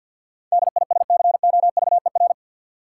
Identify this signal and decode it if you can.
Morse code, heard as a keyed tone, about 50 wpm, 700 Hz. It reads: DISCOVER
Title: Sentences from Top 1100 Words - New Word Drill - 50wpm